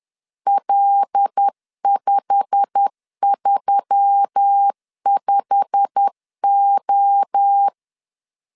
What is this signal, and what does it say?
Morse code, heard as a keyed tone, about 10 wpm, 800 Hz. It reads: L535O
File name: LEZIONI / 8 YJ45 / 8 YJ45 10wpm 2549